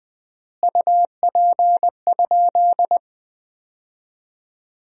Morse code, heard as a keyed tone, about 20 wpm, 700 Hz. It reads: UP?
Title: 3x Word-Spacing - Sentences from Top 200 Words - 20wpm